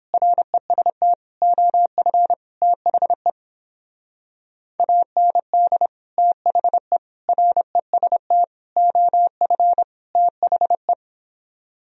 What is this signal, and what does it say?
Morse code, heard as a keyed tone, about 31 wpm, 700 Hz. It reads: REST OF THE AND THE REST OF THE